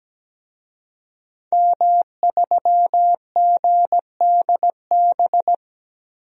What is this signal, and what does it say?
Morse code, heard as a keyed tone, about 17 wpm, 700 Hz. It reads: M3GDB
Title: DXCC-100 Call Signs - 5 Characters - 17wpm